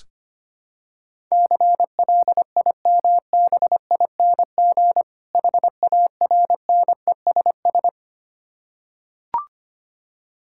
Morse code, heard as a keyed tone, about 25 wpm, 700 Hz. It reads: CLIMBING HARNESS
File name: practice / 25wpm / Difficult Words with Hints - 25wpm